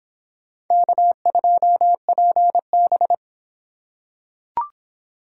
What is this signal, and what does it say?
Morse code, heard as a keyed tone, about 26 wpm, 700 Hz. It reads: K2PB